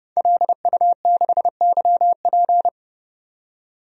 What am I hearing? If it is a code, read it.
Morse code, heard as a keyed tone, about 30 wpm, 700 Hz. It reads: LU6YP
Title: DXCC-100 Call Signs - 5 Characters - 30wpm